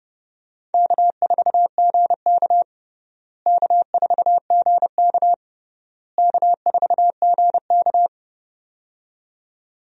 Morse code, heard as a keyed tone, about 29 wpm, 700 Hz. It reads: K4GK K4GK K4GK
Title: USA Call Signs - 4 Characters - Repeated 3x - 30wpm